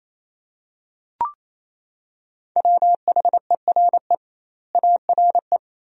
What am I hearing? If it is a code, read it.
Morse code, heard as a keyed tone, about 28 wpm, 700 Hz. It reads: WHERE ARE
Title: 2x Word-Spacing - Sentences from Top 200 Words - 28wpm